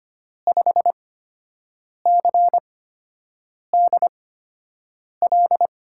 Morse code, heard as a keyed tone, about 25 wpm, spaced as for 7 wpm, 700 Hz. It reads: 5CDL
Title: USA Call Signs - 5 Characters - 8x Character Spacing - 25wpm